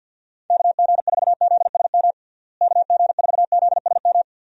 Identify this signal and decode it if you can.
Morse code, heard as a keyed tone, about 52 wpm, 700 Hz. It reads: KG4ZSM KG4ZSM
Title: USA Call Signs - 6 Characters - Repeated 2x - 50wpm